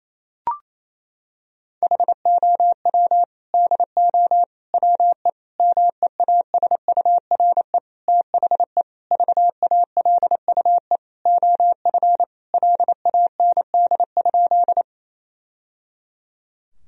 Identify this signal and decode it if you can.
Morse code, heard as a keyed tone, about 28 wpm, 700 Hz. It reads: HOW DO WE MEASURE THE VALUE OF LAND?
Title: Sentences from Top 1000 Words - Review Part 3 - 28wpm